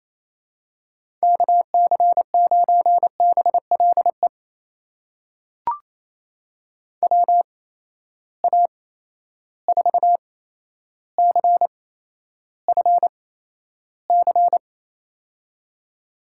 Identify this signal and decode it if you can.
Morse code, heard as a keyed tone, about 28 wpm, spaced as for 12 wpm, 700 Hz. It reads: KC9BLE WA4CFC
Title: USA Call Signs - 6 Characters - 8x Character Spacing - 28wpm